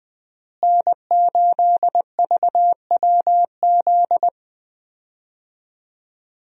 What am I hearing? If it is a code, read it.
Morse code, heard as a keyed tone, about 20 wpm, 700 Hz. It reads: N8VWZ
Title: USA Call Signs - 5 Characters - Repeated 3x - 20wpm